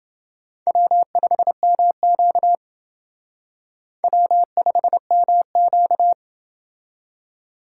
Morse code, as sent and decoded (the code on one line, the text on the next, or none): .-- ..... -- --.- / .-- ..... -- --.-
W5MQ W5MQ